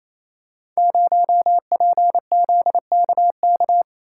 Morse code, heard as a keyed tone, about 27 wpm, 690 Hz. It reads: 0PZKK
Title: Code Groups - 5 Letter:Numbers - 28wpm